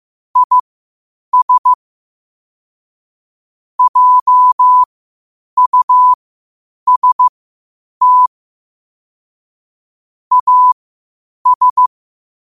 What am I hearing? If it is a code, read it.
Morse code, heard as a keyed tone, about 15 wpm, spaced as for 8 wpm, 1000 Hz. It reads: IS JUST AS
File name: qotdfiles / 8wpm